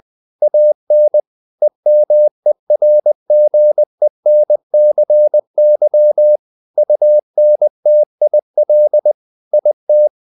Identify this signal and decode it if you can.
Morse code, heard as a keyed tone, about 20 wpm, 600 Hz. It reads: AN EMERGENCY UNTIL IT
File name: the old man and the sea part 2 20wpm